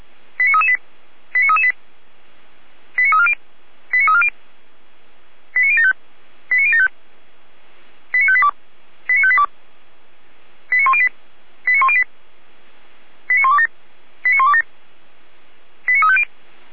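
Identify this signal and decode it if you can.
Five-tone selective call (ZVEI), heard as quick runs of five tones, tones of about 70 ms. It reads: 89338 89338 89340 89340 89075 89075 89582 89582 89108 89108 89127 89127 89350
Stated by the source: ZVEI five-tone selective call 89327 received with an SDR.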